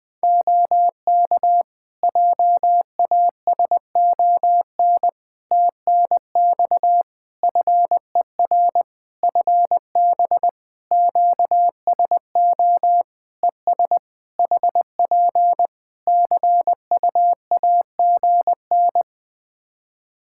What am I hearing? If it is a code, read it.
Morse code, heard as a keyed tone, about 21 wpm, 700 Hz. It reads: OK JASON TNX FER FB QSO ES HP CUAGN
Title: Practice QSO - Exchange 4 - Ending - 20wpm